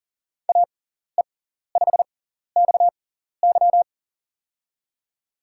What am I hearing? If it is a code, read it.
Morse code, heard as a keyed tone, about 40 wpm, spaced as for 14 wpm, 700 Hz. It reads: AE5XY
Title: USA Call Signs - 6x Character Spacing - 40wpm